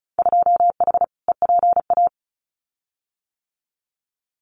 Morse code, heard as a keyed tone, about 34 wpm, 700 Hz. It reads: 2H EPA
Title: Winter Field Day Exchanges 35wpm